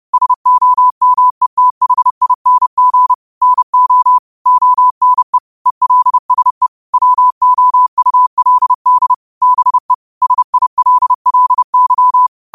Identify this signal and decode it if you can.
Morse code, heard as a keyed tone, about 30 wpm, 1000 Hz. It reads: SOMETHING NO ONE ELSE WOULD BE SILLY
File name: qotdfiles / 30wpm